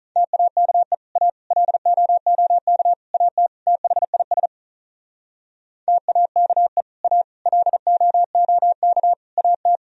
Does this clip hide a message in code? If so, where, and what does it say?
Yes: Morse code from 0.2 to 9.8 s, heard as a keyed tone, about 46 wpm, 700 Hz. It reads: TAKE A LOOK AT THIS TAKE A LOOK AT